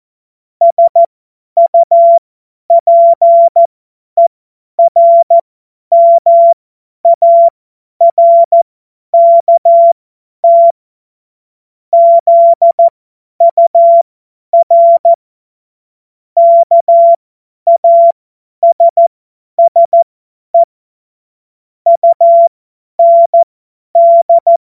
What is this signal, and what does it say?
Morse code, heard as a keyed tone, about 14 wpm, spaced as for 10 wpm, 680 Hz. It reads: SUPERMARKT ZUR KASSE UND
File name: Morsekurs / Witze / Witz 21.November 2023 50 F70